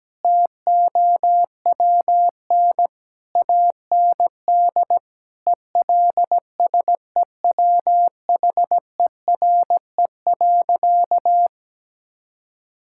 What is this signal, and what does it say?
Morse code, heard as a keyed tone, about 17 wpm, 700 Hz. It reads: TOWN AND ELSEWHERE.
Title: Sentences from Top 1600 Words 17wpm